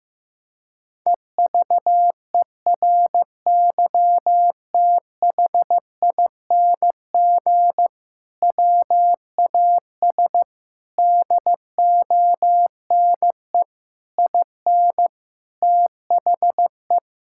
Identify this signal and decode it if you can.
Morse code, heard as a keyed tone, about 15 wpm, 700 Hz. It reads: EVERYTHING WAS DONE IN THE
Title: Sentences from Top 1600 Words 15wpm